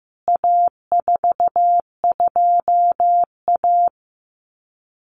Morse code, heard as a keyed tone, about 15 wpm, 700 Hz. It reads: A42A
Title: CWT Exchange 15wpm